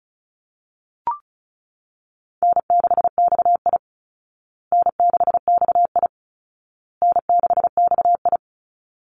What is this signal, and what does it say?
Morse code, heard as a keyed tone, about 34 wpm, 700 Hz. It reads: N6XI N6XI N6XI
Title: Easy Call Signs Repeated 3x - 35wpm